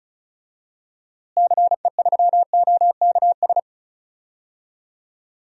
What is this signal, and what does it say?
Morse code, heard as a keyed tone, about 36 wpm, 700 Hz. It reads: CE3OKS
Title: DXCC-100 Call Signs - 6 Characters - Repeated 2x - 35wpm